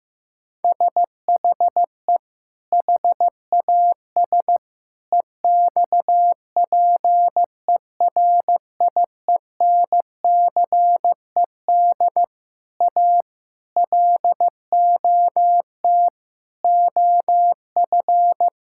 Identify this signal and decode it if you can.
Morse code, heard as a keyed tone, about 15 wpm, 700 Hz. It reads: SHE HAS EXPERIENCED A LOT OF
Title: Sentences from Top 1600 Words 15wpm